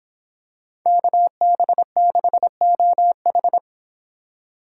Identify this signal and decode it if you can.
Morse code, heard as a keyed tone, about 26 wpm, 700 Hz. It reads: KB6OH